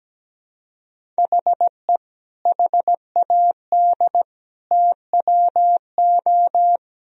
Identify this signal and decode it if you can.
Morse code, heard as a keyed tone, about 17 wpm, 700 Hz. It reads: HE HAD TWO
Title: Sentences from Top 1000 Words - Review Part 1 - 17wpm